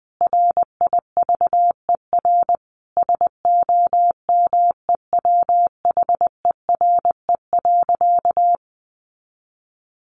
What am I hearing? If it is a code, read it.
Morse code, heard as a keyed tone, about 20 wpm, 700 Hz. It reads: RIVER SOMEWHERE.
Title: Sentences from Top 1400 Words 20wpm